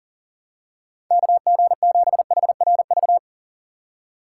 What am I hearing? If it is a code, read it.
Morse code, heard as a keyed tone, about 40 wpm, 700 Hz. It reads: KG7HRV
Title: USA Call Signs - 6 Characters - 3x Character Spacing - 40wpm